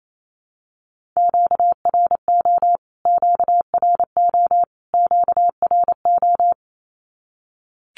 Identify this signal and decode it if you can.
Morse code, heard as a keyed tone, about 28 wpm, 700 Hz. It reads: QRO QRO QRO
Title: Q-Codes Repeated 3x 28wpm